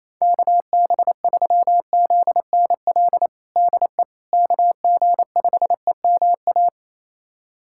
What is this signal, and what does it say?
Morse code, heard as a keyed tone, about 28 wpm, 700 Hz. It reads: KB3ZNL DE KG5EMA
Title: USA Callsign DE Callsign - 6 Characters - 28wpm